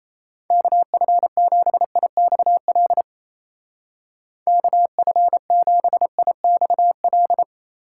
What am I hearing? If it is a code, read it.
Morse code, heard as a keyed tone, about 37 wpm, 700 Hz. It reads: KF7IXL KF7IXL